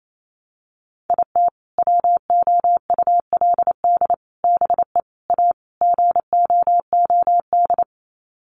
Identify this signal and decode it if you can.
Morse code, heard as a keyed tone, about 28 wpm, 700 Hz. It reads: IT WOULD BE A GOOD
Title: Sets of 5 Words Encore2 28wpm